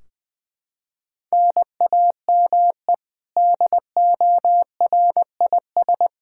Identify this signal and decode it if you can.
Morse code, heard as a keyed tone, about 20 wpm, 700 Hz. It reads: NAME DORIS
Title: Practice QSO - Exchange 1 - Beginning - 20wpm